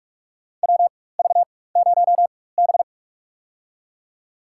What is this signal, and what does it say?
Morse code, heard as a keyed tone, about 46 wpm, spaced as for 21 wpm, 700 Hz. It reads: WV0B
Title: USA Call Signs - 4 Characters - 4x Character Spacing - 45wpm